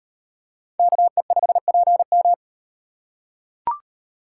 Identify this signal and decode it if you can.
Morse code, heard as a keyed tone, about 38 wpm, 700 Hz. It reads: KE5PM